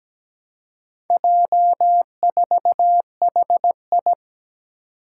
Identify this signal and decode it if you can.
Morse code, heard as a keyed tone, about 17 wpm, 700 Hz. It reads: J4HI
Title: Four Challenging Letter-Number Combinations 17wpm